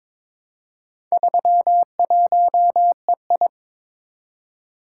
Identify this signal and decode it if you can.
Morse code, heard as a keyed tone, about 22 wpm, 700 Hz. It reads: 31EI
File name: practice / 22wpm / Four Letter-Number Combinations - 22wpm